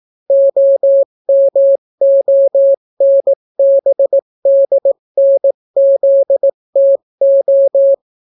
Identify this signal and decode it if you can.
Morse code, heard as a keyed tone, about 18 wpm, 550 Hz. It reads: OMONBDNZTO